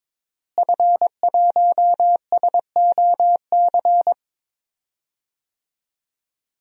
Morse code, heard as a keyed tone, about 22 wpm, 700 Hz. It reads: F1SOC
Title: DXCC-100 Call Signs - 5 Characters - Repeated 2x - 22wpm